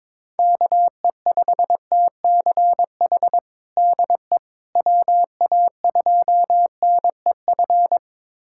Morse code, heard as a keyed tone, about 22 wpm, 700 Hz. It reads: KE5TCH DE WA2NEF
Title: USA Callsign DE Callsign 22wpm